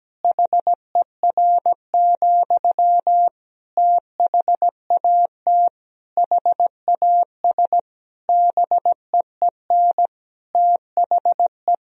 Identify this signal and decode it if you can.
Morse code, heard as a keyed tone, about 17 wpm, 700 Hz. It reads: HER, THAT HAS BEEN THE
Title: Sentences from Top 1000 Words - Review Part 3 - 17wpm